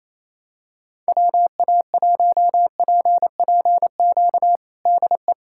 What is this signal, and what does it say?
Morse code, heard as a keyed tone, about 28 wpm, 700 Hz. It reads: WA1PPQ DE
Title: USA Callsign DE Callsign - 6 Characters - 28wpm